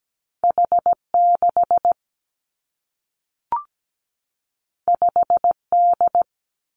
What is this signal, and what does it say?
Morse code, heard as a keyed tone, about 17 wpm, 700 Hz. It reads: H6 5D
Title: Two Challenging Letter-Number Combinations 17wpm